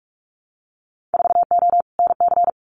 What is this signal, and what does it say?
Morse code, heard as a keyed tone, about 44 wpm, 700 Hz. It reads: 4O NC
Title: Winter Field Day Exchanges Encore 45wpm